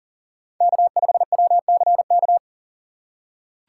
Morse code, heard as a keyed tone, about 37 wpm, 700 Hz. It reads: K5WCK
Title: USA Call Signs - 5 Characters - 2x Character Spacing - 40wpm